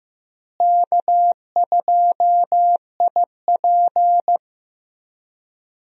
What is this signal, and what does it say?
Morse code, heard as a keyed tone, about 15 wpm, 700 Hz. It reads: K2IP
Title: USA Call Signs - 4 Characters - Repeated 3x - 15wpm